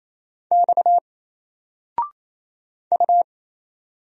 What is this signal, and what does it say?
Morse code, heard as a keyed tone, about 28 wpm, 700 Hz. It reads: X U
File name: Single-Letter-Number-28wpm